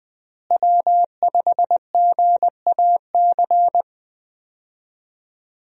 Morse code, heard as a keyed tone, about 21 wpm, 700 Hz. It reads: W5GAC
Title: USA Call Signs - 5 Characters - Repeated 2x - 20wpm